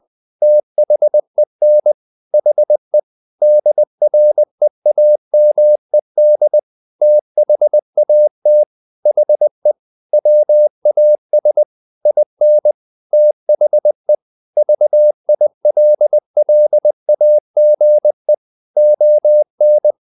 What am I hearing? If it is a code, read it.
Morse code, heard as a keyed tone, about 20 wpm, 600 Hz. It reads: THEN HE DREAMED THAT HE WAS IN THE VILLAGE ON
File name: the old man and the sea part 2 20wpm